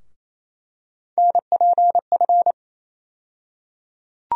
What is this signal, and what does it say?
Morse code, heard as a keyed tone, about 28 wpm, 700 Hz. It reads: NPF
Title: USA Call Sign Postfixes - 3 Letters 28wpm